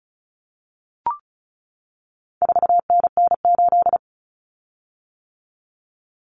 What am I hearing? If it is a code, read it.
Morse code, heard as a keyed tone, about 34 wpm, 700 Hz. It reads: 4NN8
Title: Common Cut Numbers - 4-Digit 35wpm